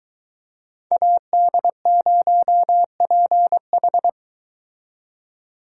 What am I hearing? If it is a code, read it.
Morse code, heard as a keyed tone, about 22 wpm, 700 Hz. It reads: AD0PH